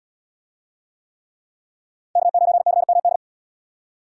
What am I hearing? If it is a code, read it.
Morse code, heard as a keyed tone, about 77 wpm, 700 Hz. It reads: K9CMD